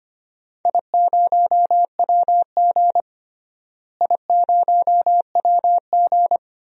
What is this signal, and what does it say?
Morse code, heard as a keyed tone, about 25 wpm, 700 Hz. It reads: I0WG I0WG
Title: DXCC-100 Call Signs - 4 Characters - Repeated 2x - 25wpm